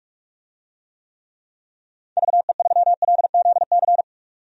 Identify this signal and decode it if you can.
Morse code, heard as a keyed tone, about 46 wpm, 700 Hz. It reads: VE3LZC
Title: DXCC-100 Call Signs - 6 Characters - 45wpm